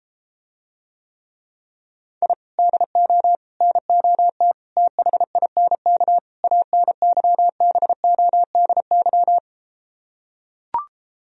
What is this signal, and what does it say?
Morse code, heard as a keyed tone, about 33 wpm, 700 Hz. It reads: I DO NOT THINK ANYBODY